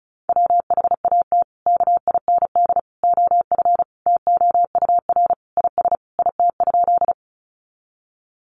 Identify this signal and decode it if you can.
Morse code, heard as a keyed tone, about 34 wpm, 700 Hz. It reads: WHAT KIND OF TOUR IS IT?
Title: Sentences from Top 1400 Words 35wpm